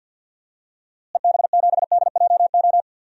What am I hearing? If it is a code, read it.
Morse code, heard as a keyed tone, about 48 wpm, 700 Hz. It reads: EB7DJO
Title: DXCC-100 Call Signs - 6 Characters - Repeated 2x - 50wpm